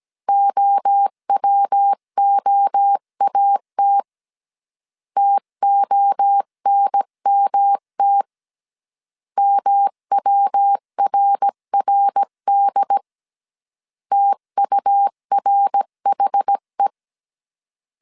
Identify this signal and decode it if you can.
Morse code, heard as a keyed tone, about 17 wpm, 800 Hz. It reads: OWOAT TONMT MWRRD TURHE
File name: CW / DET / 1 3 1500C S A 16wpm 2141